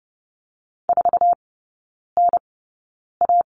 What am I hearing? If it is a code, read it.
Morse code, heard as a keyed tone, about 30 wpm, spaced as for 9 wpm, 700 Hz. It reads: 4NA
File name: Lesson 17e - Letter u - Three Characters 30wpm